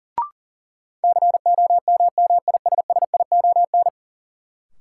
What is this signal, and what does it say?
Morse code, heard as a keyed tone, about 40 wpm, 700 Hz. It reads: COMMISSION
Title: Sentences from Top 1400 Words - New Word Drill Alt - 40wpm